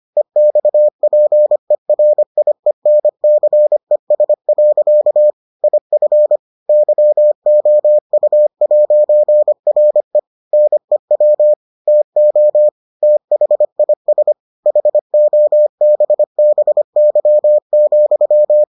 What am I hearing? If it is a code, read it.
Morse code, heard as a keyed tone, about 25 wpm, 600 Hz. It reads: EXPERIENCES. IF YOU'RE NEW TO THIS HOBBY,